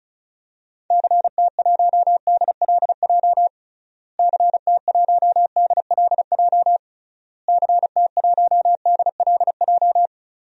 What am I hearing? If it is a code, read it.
Morse code, heard as a keyed tone, about 34 wpm, 700 Hz. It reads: CT1DLJ CT1DLJ CT1DLJ
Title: DXCC-100 Call Signs - 6 Characters - Repeated 3x - 35wpm